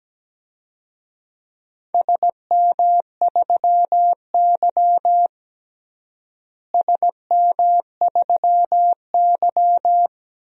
Morse code, heard as a keyed tone, about 17 wpm, 700 Hz. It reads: SM3Y SM3Y
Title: DXCC-100 Call Signs - 4 Characters - Repeated 2x - 17wpm